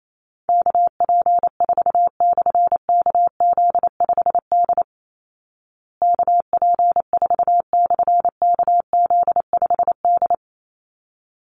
Morse code, heard as a keyed tone, about 28 wpm, 700 Hz. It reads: KP4/KZ5D KP4/KZ5D
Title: Call Signs Repeated 3x - 28wpm